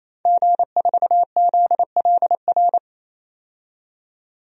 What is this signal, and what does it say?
Morse code, heard as a keyed tone, about 28 wpm, 700 Hz. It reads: G4ZLR
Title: DXCC-100 Call Signs - 5 Characters - 28wpm